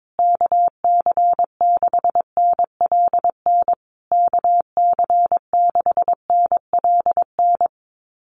Morse code, heard as a keyed tone, about 22 wpm, 700 Hz. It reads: KC6NLN KC6NLN
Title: USA Call Signs Repeated 3x 22wpm